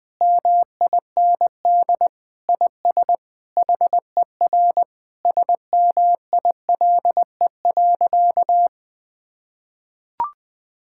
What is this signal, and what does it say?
Morse code, heard as a keyed tone, about 19 wpm, 700 Hz. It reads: MIND IS HER SMILE.
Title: Sentences with 8-Word Phrases - 20wpm